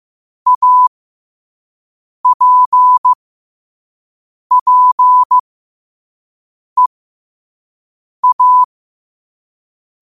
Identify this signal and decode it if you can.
Morse code, heard as a keyed tone, about 15 wpm, spaced as for 5 wpm, 1000 Hz. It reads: APPEA
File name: qotdfiles / 5wpm